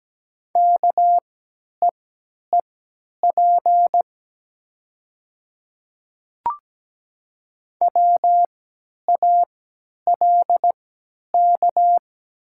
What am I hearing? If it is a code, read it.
Morse code, heard as a keyed tone, about 17 wpm, spaced as for 10 wpm, 700 Hz. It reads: KEEP WALK
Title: 3x Character Spacing - Top 500 Words - 17wpm